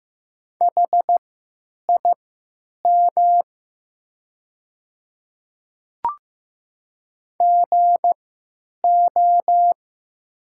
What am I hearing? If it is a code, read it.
Morse code, heard as a keyed tone, about 15 wpm, spaced as for 8 wpm, 700 Hz. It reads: HIM GO
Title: 3x Character Spacing - Top 500 Words - 15wpm